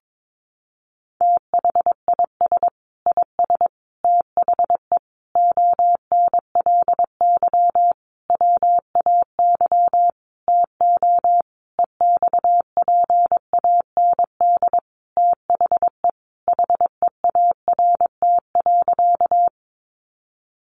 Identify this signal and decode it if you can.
Morse code, heard as a keyed tone, about 22 wpm, 700 Hz. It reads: THIS IS THE ONLY WAY TO EXPAND THE HEART.
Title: Sentences from Top 1400 Words 22wpm